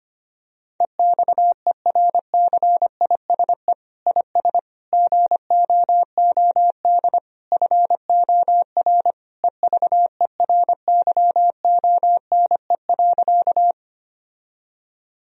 Morse code, heard as a keyed tone, about 25 wpm, 700 Hz. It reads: EXERCISE IS GOOD FOR EVERYONE.
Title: Sentences from Top 1000 Words - Review Part 4 - 25wpm